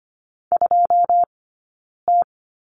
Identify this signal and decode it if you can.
Morse code, heard as a keyed tone, about 25 wpm, spaced as for 9 wpm, 690 Hz. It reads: 2T
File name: Lesson 15d - Number 5 - Two Characters 25wpm